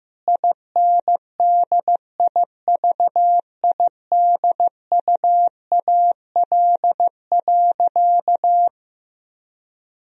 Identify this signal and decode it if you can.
Morse code, heard as a keyed tone, about 15 wpm, 700 Hz. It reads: INDIVIDUAL.
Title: Sentences from Top 1000 Words - Review Part 4 - 15wpm